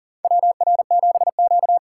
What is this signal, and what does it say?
Morse code, heard as a keyed tone, about 40 wpm, 700 Hz. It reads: WR7Q
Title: USA Call Signs - 4 Characters - 2x Character Spacing - 40wpm